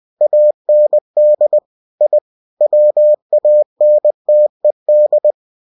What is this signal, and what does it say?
Morse code, heard as a keyed tone, about 20 wpm, 600 Hz. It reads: AND I WANTED
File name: cw-books / winnie-the-pooh / winnie-20wpm-page-0051